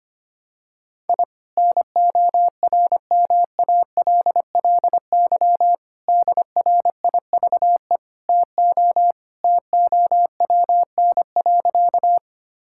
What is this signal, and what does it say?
Morse code, heard as a keyed tone, about 25 wpm, 700 Hz. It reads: I NORMALLY DRIVE TO TOWN.
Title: Sentences from Top 1000 Words - Review Part 6 - 25wpm